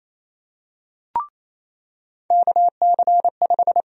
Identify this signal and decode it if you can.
Morse code, heard as a keyed tone, about 28 wpm, 700 Hz. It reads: KC5
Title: USA Call Sign Prefixes - 28wpm